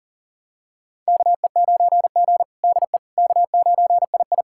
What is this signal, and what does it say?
Morse code, heard as a keyed tone, about 37 wpm, 700 Hz. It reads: KE9G DE K9II
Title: USA Callsign DE Callsign - 4 Characters - 40wpm